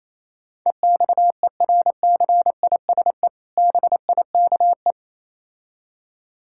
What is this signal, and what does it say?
Morse code, heard as a keyed tone, about 27 wpm, 700 Hz. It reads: EXERCISE BIKE
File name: Easy Words - Speed-Racing with Hints - 22wpm